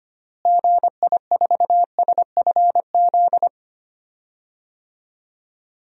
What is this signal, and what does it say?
Morse code, heard as a keyed tone, about 25 wpm, 700 Hz. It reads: GI4SFZ
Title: DXCC-100 Call Signs - 6 Characters - Repeated 2x - 25wpm